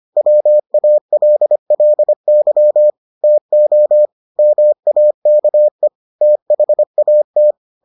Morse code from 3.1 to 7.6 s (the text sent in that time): TO MAKE THAT